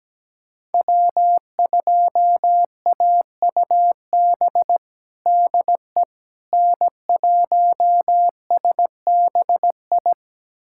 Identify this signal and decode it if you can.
Morse code, heard as a keyed tone, about 17 wpm, 700 Hz. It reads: W2AUB DE N1SBI
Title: USA Callsign DE Callsign 17wpm